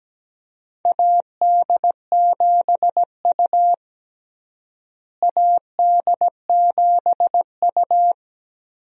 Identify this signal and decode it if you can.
Morse code, heard as a keyed tone, about 17 wpm, 700 Hz. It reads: AD7U AD7U
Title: USA Call Signs - 4 Characters - Repeated 3x - 17wpm